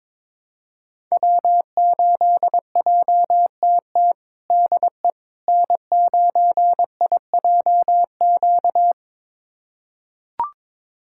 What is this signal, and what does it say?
Morse code, heard as a keyed tone, about 22 wpm, 700 Hz. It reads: W8JTT DE N9IJQ
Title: USA Callsign DE Callsign - 5 Characters - 22wpm